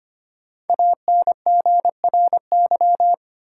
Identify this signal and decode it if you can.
Morse code, heard as a keyed tone, about 25 wpm, 700 Hz. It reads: ANGRY
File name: 5 Letter Words - First 3 Letters Given - 25wpm